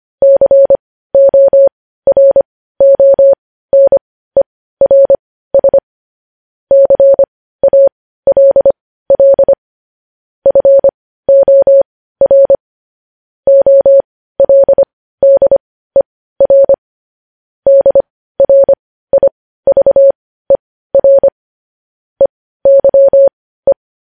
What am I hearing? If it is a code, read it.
Morse code, heard as a keyed tone, about 25 wpm, spaced as for 15 wpm, 550 Hz. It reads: CORONERS CALL FOR OLDER DRIVER EYE
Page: news-headlines-in-morse-code-at-15-wpm